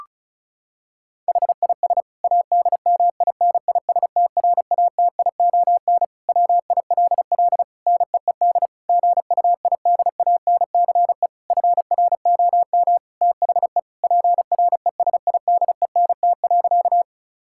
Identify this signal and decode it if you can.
Morse code, heard as a keyed tone, about 36 wpm, 690 Hz. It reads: HIS ADMINISTRATION WILL NEED GUIDANCE FROM THE PRESIDENT.
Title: Sentences from Top 1600 Words 35wpm